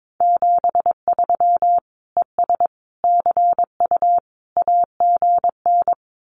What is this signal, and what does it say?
Morse code, heard as a keyed tone, about 22 wpm, 700 Hz. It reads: 73 ES CU AGN
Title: Common QSO Phrases Encore 22wpm